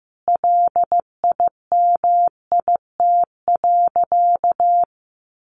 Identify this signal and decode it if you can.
Morse code, heard as a keyed tone, about 15 wpm, 700 Hz. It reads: LIMIT.
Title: Sentences from Top 1500 Words 15wpm